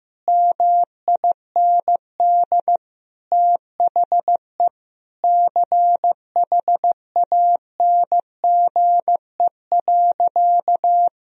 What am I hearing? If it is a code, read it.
Morse code, heard as a keyed tone, about 16 wpm, 700 Hz. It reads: MIND THE CHANGE.
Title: Sentences from Top 1000 Words - Review Part 1 - 15wpm